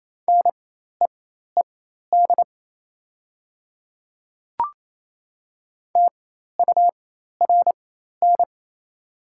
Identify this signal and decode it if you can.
Morse code, heard as a keyed tone, about 28 wpm, spaced as for 13 wpm, 690 Hz. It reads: NEED TURN
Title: 4x Character Spacing - Top 300 Words - 28wpm